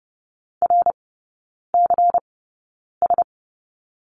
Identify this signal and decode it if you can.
Morse code, heard as a keyed tone, about 30 wpm, spaced as for 9 wpm, 700 Hz. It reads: RCS
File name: Lesson 19e - Letter w - Three Characters 30wpm